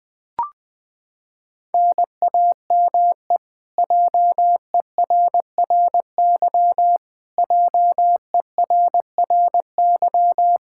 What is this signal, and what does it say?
Morse code, heard as a keyed tone, about 20 wpm, 700 Hz. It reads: NAME JERRY JERRY
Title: Practice QSO - Exchange 1 - Beginning - 20wpm